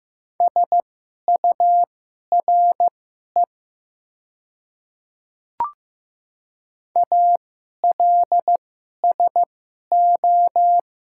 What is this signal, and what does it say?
Morse code, heard as a keyed tone, about 15 wpm, spaced as for 11 wpm, 700 Hz. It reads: SURE ALSO
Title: 2x Character Spacing - Top 300 Words - 15wpm